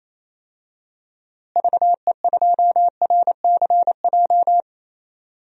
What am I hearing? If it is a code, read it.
Morse code, heard as a keyed tone, about 28 wpm, 700 Hz. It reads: VE2RCJ
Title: DXCC-100 Call Signs - 6 Characters - Repeated 2x - 28wpm